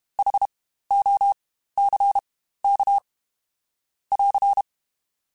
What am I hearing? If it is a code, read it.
Morse code, heard as a keyed tone, about 32 wpm, spaced as for 15 wpm, 800 Hz. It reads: HOCK <AR>
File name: Spaced Alphas / Spaced Alphas mp3s / Hs sp @30